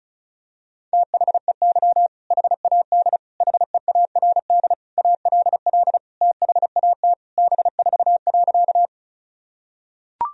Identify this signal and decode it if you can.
Morse code, heard as a keyed tone, about 36 wpm, 700 Hz. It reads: THEY HAD HEARD ALL THAT B4.
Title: Sentences from Top 400 Words and Abbreviations 35wpm